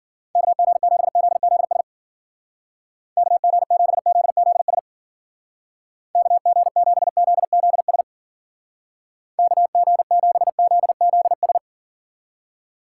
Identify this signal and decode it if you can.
Morse code, heard as a keyed tone, about 48 wpm, 700 Hz. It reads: KG7ZZS KG7ZZS KG7ZZS KG7ZZS
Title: Speed Racing - USA 2x3 Call Signs 40wpm